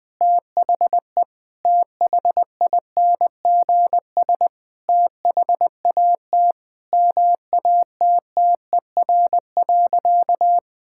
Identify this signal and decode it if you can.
Morse code, heard as a keyed tone, about 19 wpm, 700 Hz. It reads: THE THINGS THAT MATTER.
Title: Sentences with 5-Word Phrases - Spoken Beginning - 20wpm